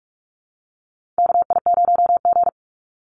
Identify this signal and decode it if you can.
Morse code, heard as a keyed tone, about 44 wpm, 690 Hz. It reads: KI0G
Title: NA SOTA Chaser Callsigns 45wpm